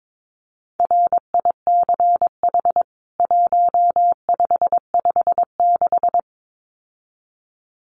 Morse code, heard as a keyed tone, about 22 wpm, 700 Hz. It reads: RICH 1556
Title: CWT Exchange 22wpm